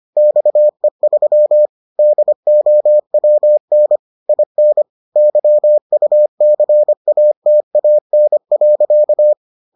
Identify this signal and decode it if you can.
Morse code, heard as a keyed tone, about 25 wpm, 600 Hz. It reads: XE3 DOWN IN YUCATAN.